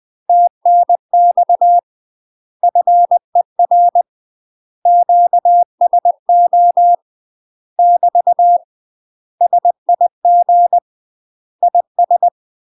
Morse code, heard as a keyed tone, about 20 wpm, 700 Hz. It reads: TNX FER QSO <BT> SIG IS
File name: UK MASTER QSOs TRAINING / 20WPM / UK MASTER QSOs 0073